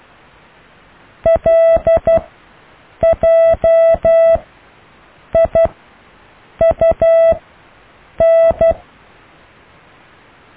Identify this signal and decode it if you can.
Morse code, heard as a keyed tone, about 12 wpm, 660 Hz. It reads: L J I U N
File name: cw / IJLMU